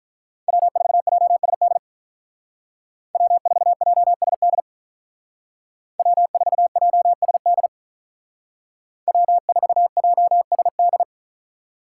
Morse code, heard as a keyed tone, about 47 wpm, 700 Hz. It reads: W4JSD W4JSD W4JSD W4JSD